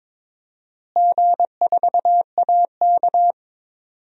Morse code, heard as a keyed tone, about 22 wpm, 700 Hz. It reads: G4AK
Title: DXCC-100 Call Signs - 4 Characters - Repeated 2x - 22wpm